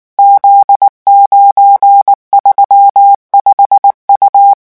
Morse code, heard as a keyed tone, about 19 wpm, 800 Hz. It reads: Z935U
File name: morse mp3 / revision19-vitesse 18 mots